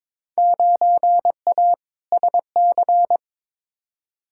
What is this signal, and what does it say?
Morse code, heard as a keyed tone, about 21 wpm, 700 Hz. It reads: 9A SC
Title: Field Day Exchanges Encore - 22wpm